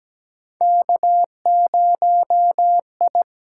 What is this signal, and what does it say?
Morse code, heard as a keyed tone, about 17 wpm, 700 Hz. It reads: K0I
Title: USA Call Signs - 4 Characters - 8x Character Spacing - 17wpm